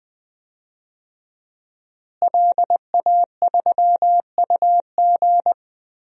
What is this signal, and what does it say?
Morse code, heard as a keyed tone, about 21 wpm, 700 Hz. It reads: LA3UG
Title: DXCC-100 Call Signs - 5 Characters - Repeated 2x - 20wpm